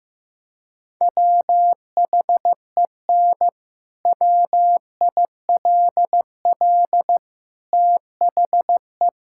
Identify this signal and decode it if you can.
Morse code, heard as a keyed tone, about 15 wpm, 700 Hz. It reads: WHEN WILL THE
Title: Sentences from Top 1000 Words - Review Part 6 - 15wpm